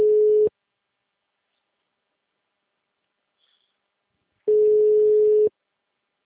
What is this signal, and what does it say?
Ringback tone (Europe, 425 Hz).
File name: out- -2001- - - .4